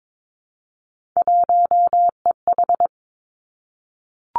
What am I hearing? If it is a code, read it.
Morse code, heard as a keyed tone, about 22 wpm, 700 Hz. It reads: 1EH
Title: Three Challenging Letter-Number Combinations 22wpm